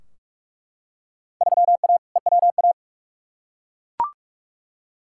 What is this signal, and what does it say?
Morse code, heard as a keyed tone, about 46 wpm, 700 Hz. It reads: 3A EWA
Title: Field Day Exchanges - 45wpm